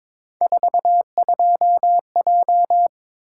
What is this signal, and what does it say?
Morse code, heard as a keyed tone, about 22 wpm, 700 Hz. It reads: 42J
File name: practice / 22wpm / Three Letter-Number Combinations - 22wpm